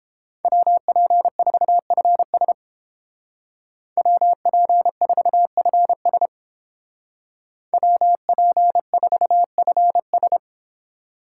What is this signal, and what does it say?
Morse code, heard as a keyed tone, about 29 wpm, 700 Hz. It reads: WP4FS WP4FS WP4FS